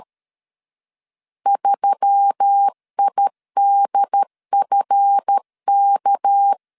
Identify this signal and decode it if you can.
Morse code, heard as a keyed tone, about 13 wpm, 800 Hz. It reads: 3IDFK
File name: LEZIONI / 7 GF9Z / 7 GF9 12wpm 2319